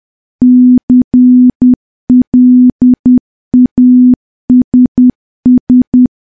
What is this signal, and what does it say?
Morse code, heard as a keyed tone, about 10 wpm, 250 Hz. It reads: CLASS
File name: morse / morse-training-10WPM